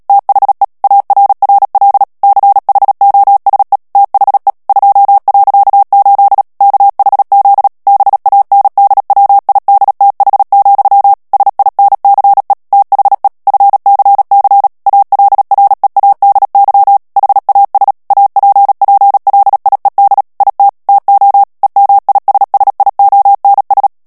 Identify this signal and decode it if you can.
Morse code, heard as a keyed tone, about 37 wpm, 800 Hz. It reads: THE ARRL CHOSE THE 2.8 KHZ BANDWIDTH, SINCE THE FCC ALREADY HAS APPLIED IT TO EMISSIONS
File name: morse mp3 / ANGLAIS texte clair 3 vitesse 35 mots minute